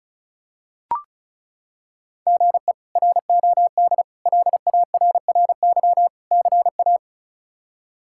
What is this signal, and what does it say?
Morse code, heard as a keyed tone, about 34 wpm, 700 Hz. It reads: GE ROD LARRY CA
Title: K1USN SST QSO Practice - 35wpm